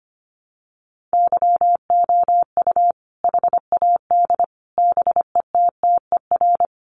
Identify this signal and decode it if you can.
Morse code, heard as a keyed tone, about 25 wpm, 700 Hz. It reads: YOU HAD BETTER
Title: Sentences from Top 400 Words 25wpm